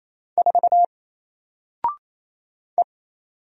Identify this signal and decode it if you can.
Morse code, heard as a keyed tone, about 28 wpm, 700 Hz. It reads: 4 E
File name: Single-Letter-Number-28wpm